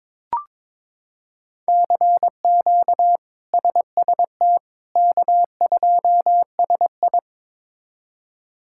Morse code, heard as a keyed tone, about 22 wpm, 700 Hz. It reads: CQ SST K2SI
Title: K1USN SST QSO Practice - 22wpm